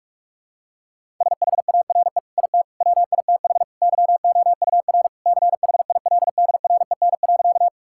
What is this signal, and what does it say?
Morse code, heard as a keyed tone, about 42 wpm, 700 Hz. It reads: SHARE IT WITH YOUR CHILDREN.
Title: Sentences from Top 1000 Words - Review Part 6 - 45wpm